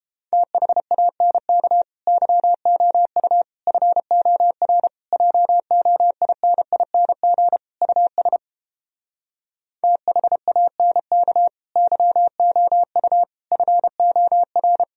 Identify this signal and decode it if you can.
Morse code, heard as a keyed tone, about 29 wpm, 700 Hz. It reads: THANK YOU FOR JOINING US THANK YOU FOR